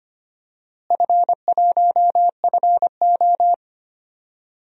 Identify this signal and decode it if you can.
Morse code, heard as a keyed tone, about 25 wpm, 700 Hz. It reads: F1FO
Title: DXCC-100 Call Signs - 4 Characters - Repeated 3x - 25wpm